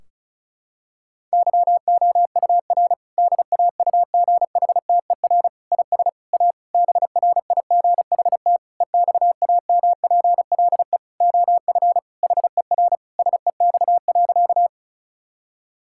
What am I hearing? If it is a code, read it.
Morse code, heard as a keyed tone, about 36 wpm, 700 Hz. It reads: YOUR DAUGHTER IS A BRIGHT EXAMPLE OF HER SEX.
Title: Sentences from Top 1600 Words 35wpm